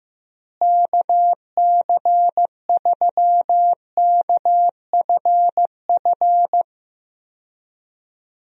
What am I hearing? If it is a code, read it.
Morse code, heard as a keyed tone, about 15 wpm, 700 Hz. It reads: KC3KFF
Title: USA Call Signs - 6 Characters - Repeated 2x - 15wpm